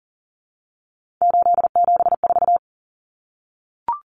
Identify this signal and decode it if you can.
Morse code, heard as a keyed tone, about 40 wpm, 700 Hz. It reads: 874
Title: Numbers - 3-Digit 40wpm